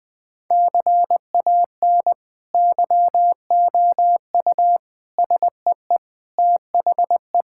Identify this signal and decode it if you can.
Morse code, heard as a keyed tone, about 20 wpm, 700 Hz. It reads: CAN YOU SEE THE
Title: Sentences with 2-Word Phrases - 20wpm